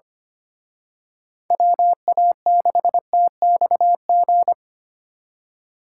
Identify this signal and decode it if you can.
Morse code, heard as a keyed tone, about 29 wpm, 700 Hz. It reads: WA6TXG
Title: Speed Racing - USA 2x3 Call Signs 25wpm